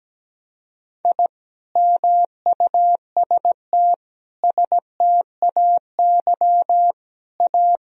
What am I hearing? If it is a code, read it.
Morse code, heard as a keyed tone, about 17 wpm, 700 Hz. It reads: I MUST STAY A
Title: Sentences from Top 1000 Words - Review Part 1 - 17wpm